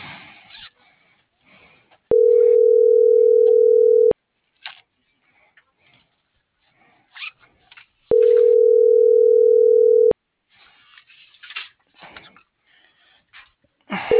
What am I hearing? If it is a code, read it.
Ringback tone (North America), 440 + 480 Hz.